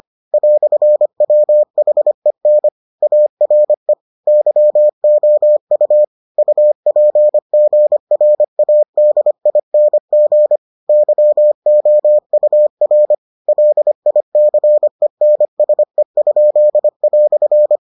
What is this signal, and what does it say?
Morse code, heard as a keyed tone, about 25 wpm, 600 Hz. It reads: "WHEN ARE YOU UPGRADING YOUR LICENSE?"